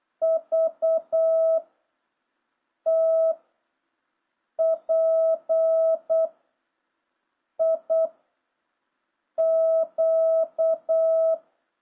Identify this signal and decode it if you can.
Morse code, heard as a keyed tone, about 8 wpm, 650 Hz. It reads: V T P I Q